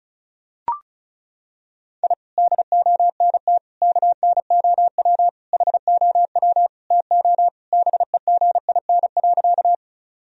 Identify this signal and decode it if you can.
Morse code, heard as a keyed tone, about 36 wpm, 700 Hz. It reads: I DONT KNOW HOW TO BEGIN.
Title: Sentences from Top 1000 Words - Review Part 1 - 35wpm